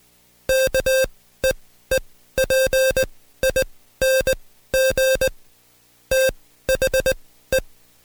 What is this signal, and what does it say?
Morse code, heard as a keyed tone, about 20 wpm, spaced as for 14 wpm, 530 Hz. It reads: KEEPING THE
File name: Code Quick / AudioDownld2 / Seg133-Stepper6 13WPM